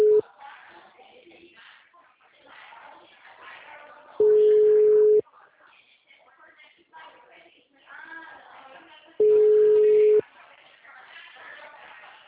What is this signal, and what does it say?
Ringback tone (Europe, 425 Hz), with speech.